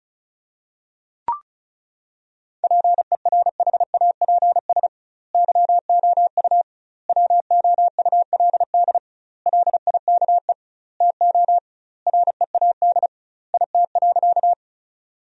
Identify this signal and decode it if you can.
Morse code, heard as a keyed tone, about 36 wpm, 700 Hz. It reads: PERHAPS YOU WOULD LIKE TO READ IT.
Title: 2x Word-Spacing - Sentences from Top 500 Words - 35wpm